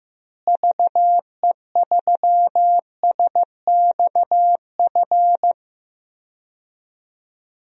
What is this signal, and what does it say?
Morse code, heard as a keyed tone, about 16 wpm, 700 Hz. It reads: VE3SXF
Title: DXCC-100 Call Signs - 6 Characters - Repeated 2x - 15wpm